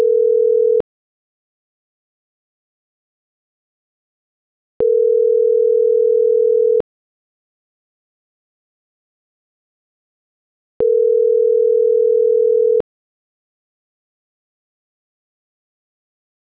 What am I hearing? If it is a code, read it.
Ringback tone (North America), 440 + 480 Hz.